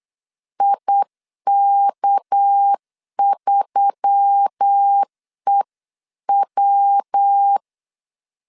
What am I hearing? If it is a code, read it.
Morse code, heard as a keyed tone, about 8 wpm, 800 Hz. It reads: IK3EW
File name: LEZIONI / 9 2 8wpm 3437